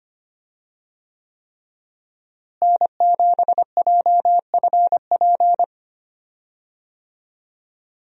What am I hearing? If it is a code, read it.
Morse code, heard as a keyed tone, about 25 wpm, 700 Hz. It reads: N7JFP
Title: USA Call Signs - 5 Characters - Repeated 3x - 25wpm